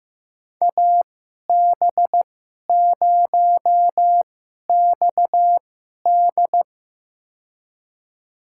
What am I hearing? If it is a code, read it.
Morse code, heard as a keyed tone, about 15 wpm, spaced as for 11 wpm, 700 Hz. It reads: AB0XD
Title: USA Call Signs - 2x Character Spacing - 15wpm